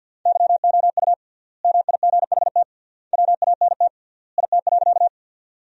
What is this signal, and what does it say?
Morse code, heard as a keyed tone, about 50 wpm, 700 Hz. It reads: YOU MIGHT WANT IT.
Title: 3x Word-Spacing - Sentences from Top 300 Words - 50wpm